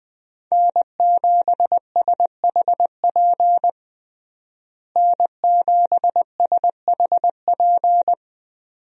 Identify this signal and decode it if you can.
Morse code, heard as a keyed tone, about 20 wpm, 700 Hz. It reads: N7SHP N7SHP
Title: USA Call Signs - 5 Characters - Repeated 2x - 20wpm